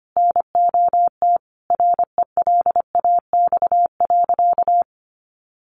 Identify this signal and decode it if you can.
Morse code, heard as a keyed tone, about 25 wpm, 700 Hz. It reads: NOT RELAX.
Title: Sentences from Top 1400 Words 25wpm